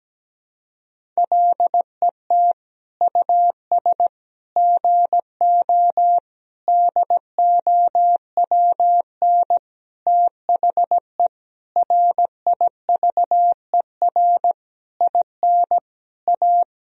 Morse code, heard as a keyed tone, about 17 wpm, 700 Hz. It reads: LET US GO DOWN THE RIVER IN A
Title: Sentences from Top 1000 Words - Review Part 3 - 17wpm